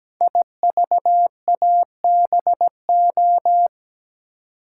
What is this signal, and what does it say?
Morse code, heard as a keyed tone, about 17 wpm, 700 Hz. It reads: IVABO
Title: Code Groups - 5 Letters - 17wpm